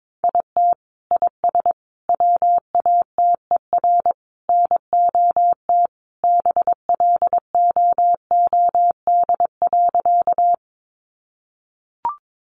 Morse code, heard as a keyed tone, about 22 wpm, 700 Hz. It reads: IT IS WATER NOT BLOOD.
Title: Sentences from Top 1100 Words 22wpm